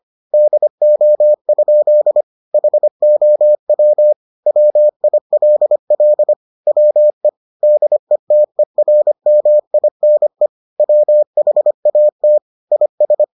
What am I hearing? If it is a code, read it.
Morse code, heard as a keyed tone, about 25 wpm, 600 Hz. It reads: DO? HOW WILL WE DETERMINE WHAT IS